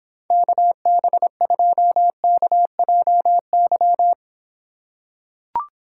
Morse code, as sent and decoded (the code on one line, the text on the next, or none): -.- -... ..--- -.- .--- -.--
KB2KJY